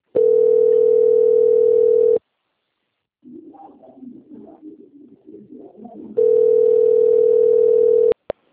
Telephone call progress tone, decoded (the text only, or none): ringback tone